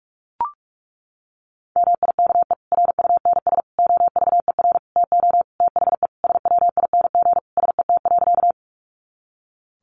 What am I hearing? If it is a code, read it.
Morse code, heard as a keyed tone, about 44 wpm, 700 Hz. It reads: MIKE RUNS OVER TO THE SWING SET.
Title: Additional Easy Phrases 45wpm